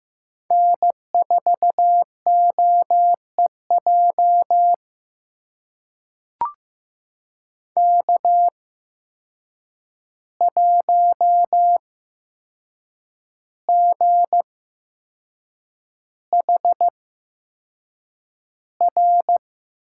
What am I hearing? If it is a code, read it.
Morse code, heard as a keyed tone, about 15 wpm, spaced as for 6 wpm, 700 Hz. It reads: N4OEJ K1GHR
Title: USA Call Signs - 5 Characters - 8x Character Spacing - 15wpm